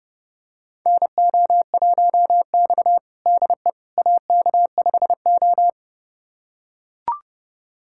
Morse code, heard as a keyed tone, about 30 wpm, 700 Hz. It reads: NO1X DE AK5O
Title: USA Callsign DE Callsign - 4 Characters - 30wpm